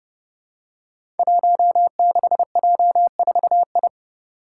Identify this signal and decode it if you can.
Morse code, heard as a keyed tone, about 30 wpm, 700 Hz. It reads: 16J4I
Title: Five Challenging Letter-Number Combinations 30wpm